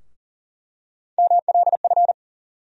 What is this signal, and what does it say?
Morse code, heard as a keyed tone, about 38 wpm, 700 Hz. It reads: MLF
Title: USA Call Sign Postfixes - 3 Letters 40wpm